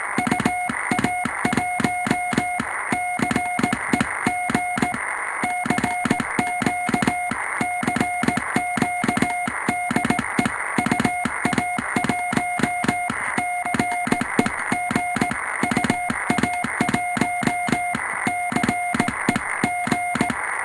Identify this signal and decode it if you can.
Morse code, heard as a keyed tone, about 18 wpm, 720 Hz, with noise. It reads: UA1CEG CQ CQ DE UA1CEG UA1CEG
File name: attachments / 20.07.2019 manipuljacia home made